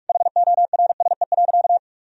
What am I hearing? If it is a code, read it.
Morse code, heard as a keyed tone, about 47 wpm, 700 Hz. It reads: HORSE.
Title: Sentences from Top 1000 Words - Review Part 4 - 45wpm